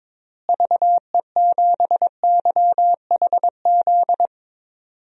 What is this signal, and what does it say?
Morse code, heard as a keyed tone, about 22 wpm, 700 Hz. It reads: VE7YHZ
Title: DXCC-100 Call Signs - 6 Characters - Repeated 3x - 22wpm